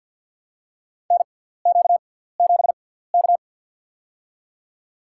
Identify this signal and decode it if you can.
Morse code, heard as a keyed tone, about 47 wpm, spaced as for 17 wpm, 700 Hz. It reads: NQ7K
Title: USA Call Signs - 4 Characters - 6x Character Spacing - 50wpm